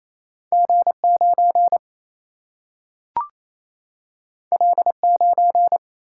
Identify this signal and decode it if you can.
Morse code, heard as a keyed tone, about 28 wpm, 700 Hz. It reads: G 9 L 9
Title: Single-Letter Single-Number 28wpm